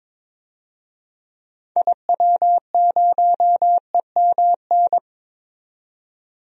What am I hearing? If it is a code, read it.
Morse code, heard as a keyed tone, about 21 wpm, 700 Hz. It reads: IW0EMN
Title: DXCC-100 Call Signs - 6 Characters - 22wpm